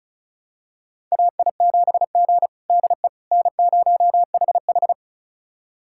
Morse code, heard as a keyed tone, about 36 wpm, 700 Hz. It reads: AI7G DE N0HH
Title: USA Callsign DE Callsign - 4 Characters - 35wpm